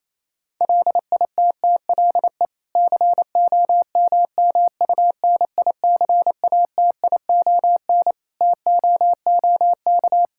Morse code, heard as a keyed tone, about 28 wpm, 700 Hz. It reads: LITTLE COMMUNICATION TOOK
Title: Sentences from Top 1000 Words - Review Part 6 - 28wpm